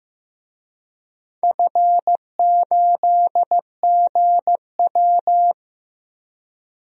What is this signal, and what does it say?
Morse code, heard as a keyed tone, about 15 wpm, 700 Hz. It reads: F8GW
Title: DXCC-100 Call Signs - 4 Characters - 15wpm